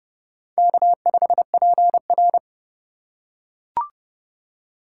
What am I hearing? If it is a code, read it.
Morse code, heard as a keyed tone, about 30 wpm, 700 Hz. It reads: K5PR
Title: USA Call Signs - 8x Character Spacing - 30wpm